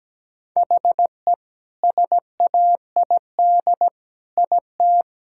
Morse code, heard as a keyed tone, about 17 wpm, 700 Hz. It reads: HE SAID IT
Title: Sentences from Top 1600 Words 17wpm